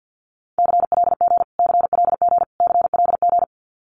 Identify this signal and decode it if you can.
Morse code, heard as a keyed tone, about 50 wpm, 700 Hz. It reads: CLG CLG CLG
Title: Abbreviations Repeated 3x - 50wpm